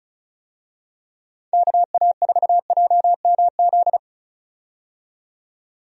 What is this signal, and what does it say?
Morse code, heard as a keyed tone, about 36 wpm, 700 Hz. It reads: KA4JMZ
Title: USA Call Signs - 6 Characters - 35wpm